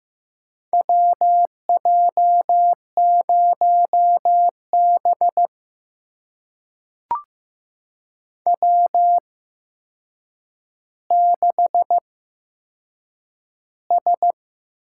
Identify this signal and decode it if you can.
Morse code, heard as a keyed tone, about 15 wpm, spaced as for 7 wpm, 700 Hz. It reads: WJ0B W6S
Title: USA Call Signs - 4 Characters - 8x Character Spacing - 15wpm